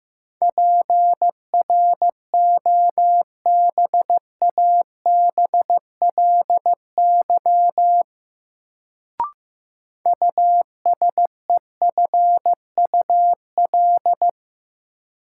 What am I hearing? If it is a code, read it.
Morse code, heard as a keyed tone, about 15 wpm, 700 Hz. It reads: PROBABLY USEFUL
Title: Sentences from Top 700 Words - New Word Drill - 15wpm